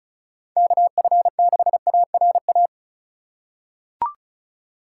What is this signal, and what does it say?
Morse code, heard as a keyed tone, about 36 wpm, 700 Hz. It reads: KF6ARA
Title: USA Call Signs - 6 Characters - 3x Character Spacing - 35wpm